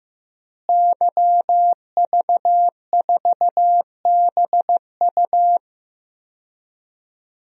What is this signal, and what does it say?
Morse code, heard as a keyed tone, about 15 wpm, 700 Hz. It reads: YV4BU
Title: DXCC-100 Call Signs - 5 Characters - Repeated 3x - 15wpm